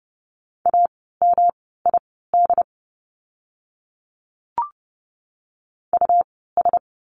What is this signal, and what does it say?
Morse code, heard as a keyed tone, about 30 wpm, spaced as for 17 wpm, 700 Hz. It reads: AMID US
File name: course / 30:10wpm / 108 Lesson 18f - Letter m - Words 30wpm